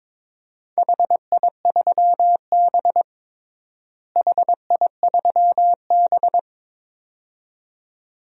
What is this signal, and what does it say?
Morse code, heard as a keyed tone, about 21 wpm, 700 Hz. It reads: HI3B HI3B
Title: DXCC-100 Call Signs - 4 Characters - Repeated 2x - 22wpm